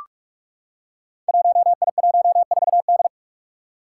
Morse code, heard as a keyed tone, about 43 wpm, 700 Hz. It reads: 1I14D
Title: Five Challenging Letter-Number Combinations 45wpm